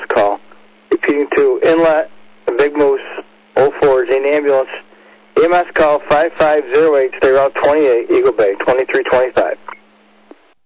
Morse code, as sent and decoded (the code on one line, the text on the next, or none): none